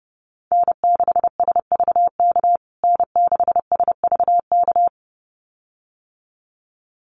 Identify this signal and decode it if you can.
Morse code, heard as a keyed tone, about 30 wpm, 700 Hz. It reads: N6SVK N6SVK
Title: USA Call Signs Repeated 2x 30wpm